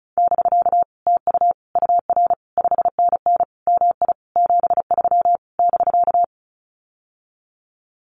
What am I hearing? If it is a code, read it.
Morse code, heard as a keyed tone, about 36 wpm, 700 Hz. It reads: <BK> TU UR 5NN MI 73 <BK>
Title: Common QSO Phrases Encore 35wpm